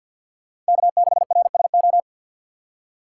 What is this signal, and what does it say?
Morse code, heard as a keyed tone, about 62 wpm, 700 Hz. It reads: K6RSO
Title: Speed Racing - USA 1x3 Call Signs 50wpm